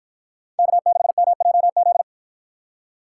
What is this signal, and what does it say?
Morse code, heard as a keyed tone, about 53 wpm, 700 Hz. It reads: K6MJZ